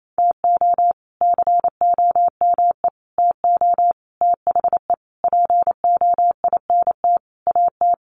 Morse code, heard as a keyed tone, about 28 wpm, 700 Hz. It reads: TO COME TO THE POINT AT
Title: Sentences from Top 900 Words 28wpm